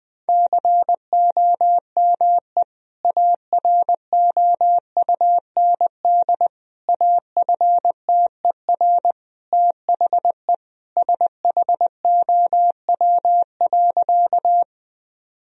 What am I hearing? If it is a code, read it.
Morse code, heard as a keyed tone, about 20 wpm, 700 Hz. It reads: COME AROUND AFTER THE SHOW.
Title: Sentences from Top 1000 Words - Review Part 3 - 20wpm